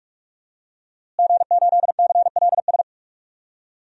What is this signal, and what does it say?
Morse code, heard as a keyed tone, about 46 wpm, 700 Hz. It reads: G8CLS
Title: DXCC-100 Call Signs - 5 Characters - Repeated 3x - 45wpm